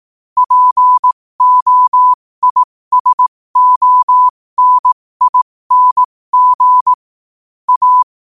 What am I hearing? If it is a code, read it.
Morse code, heard as a keyed tone, about 18 wpm, spaced as for 15 wpm, 1000 Hz. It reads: POISONING A
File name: qotdfiles / 15wpm